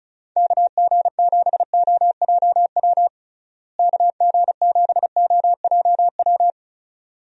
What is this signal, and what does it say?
Morse code, heard as a keyed tone, about 36 wpm, 700 Hz. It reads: KG7OJW KG7OJW
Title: USA Call Signs - 6 Characters - Repeated 2x - 35wpm